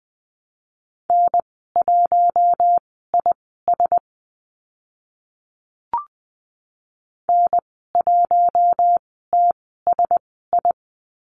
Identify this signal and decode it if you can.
Morse code, heard as a keyed tone, about 20 wpm, spaced as for 14 wpm, 700 Hz. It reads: N1IS N1TSI
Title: 042 Lesson 8g - Number 1 - Call Signs 20wpm